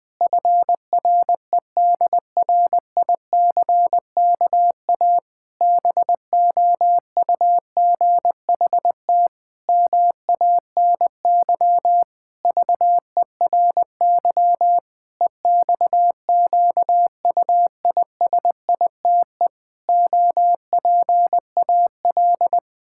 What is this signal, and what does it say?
Morse code, heard as a keyed tone, about 20 wpm, 700 Hz. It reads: FREDRICKA BOUGHT MANY VERY EXQUISITE OPAL
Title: Pangram Challenge v2 - 20wpm